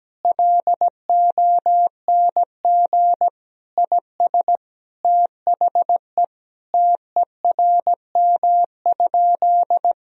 Morse code, heard as a keyed tone, about 17 wpm, 700 Hz. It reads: LONG IS THE TERM?
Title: Sentences from Top 1000 Words - Review Part 4 - 17wpm